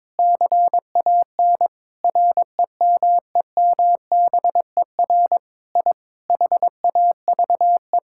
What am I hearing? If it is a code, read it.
Morse code, heard as a keyed tone, about 22 wpm, 700 Hz. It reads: CAN REMEMBER I HAVE
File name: practice / 22wpm / Sentences with 8-Word Phrases - 22wpm